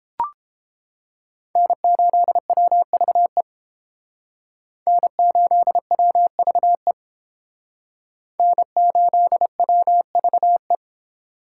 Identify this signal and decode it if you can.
Morse code, heard as a keyed tone, about 30 wpm, 700 Hz. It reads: N8WVE N8WVE N8WVE